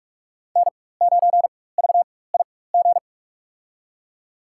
Morse code, heard as a keyed tone, about 43 wpm, spaced as for 21 wpm, 700 Hz. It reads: N9VIG
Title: USA Call Signs - 4x Character Spacing - 45wpm